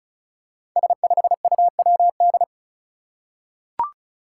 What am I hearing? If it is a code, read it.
Morse code, heard as a keyed tone, about 36 wpm, 700 Hz. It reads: S5UWD
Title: Five Challenging Letter-Number Combinations 35wpm